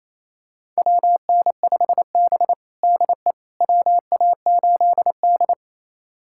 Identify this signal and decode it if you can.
Morse code, heard as a keyed tone, about 28 wpm, 700 Hz. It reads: WN5B DE WA8D
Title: USA Callsign DE Callsign - 4 Characters - 28wpm